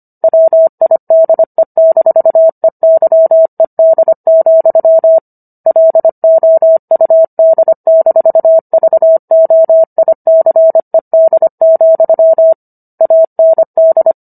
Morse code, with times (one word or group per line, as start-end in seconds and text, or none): 0.2-5.2 WIDE-EYED,
5.7-12.5 LOUD-VOICED,
13.0-14.1 AND